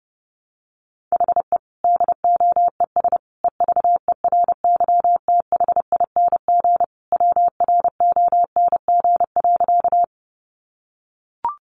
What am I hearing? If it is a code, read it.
Morse code, heard as a keyed tone, about 30 wpm, 700 Hz. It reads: HE DOES EVERYTHING WRONG.
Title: Sentences from Top 900 Words 30wpm